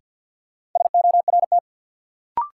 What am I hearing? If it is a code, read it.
Morse code, heard as a keyed tone, about 53 wpm, 700 Hz. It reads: SORT
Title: Sets of 2 Words - New Word Drill - 50wpm